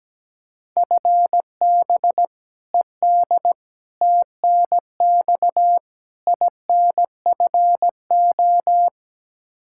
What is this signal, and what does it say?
Morse code, heard as a keyed tone, about 17 wpm, 700 Hz. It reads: FB ED TNX INFO
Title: K1USN SST QSO Practice 17wpm